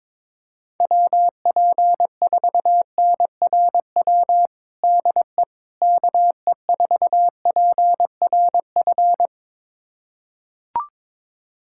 Morse code, heard as a keyed tone, about 22 wpm, 700 Hz. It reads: WP4NRW DE KE4PRF
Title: USA Callsign DE Callsign - 6 Characters - 22wpm